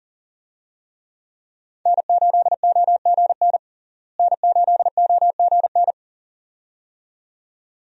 Morse code, heard as a keyed tone, about 40 wpm, 700 Hz. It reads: N8OGN N8OGN
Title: USA Call Signs - 5 Characters - Repeated 2x - 40wpm